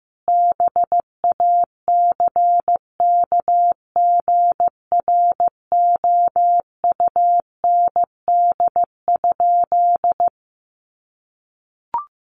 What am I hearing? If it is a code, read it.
Morse code, heard as a keyed tone, about 15 wpm, 700 Hz. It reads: BACKGROUND?
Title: Sentences from Top 1300 Words 15wpm